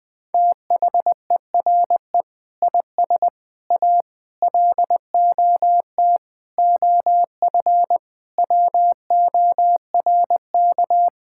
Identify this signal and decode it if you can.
Morse code, heard as a keyed tone, about 21 wpm, 700 Hz. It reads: THERE IS A LOT OF WORK
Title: Sentences with 7-Word Phrases - 20wpm